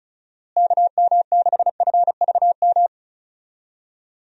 Morse code, heard as a keyed tone, about 36 wpm, 700 Hz. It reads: KM6FVM
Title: USA Call Signs - 6 Characters - Repeated 2x - 35wpm